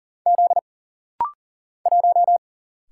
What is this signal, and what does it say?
Morse code, heard as a keyed tone, about 40 wpm, 700 Hz. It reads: Z 1
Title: Single Letter-Number - Rapid-Fire 40wpm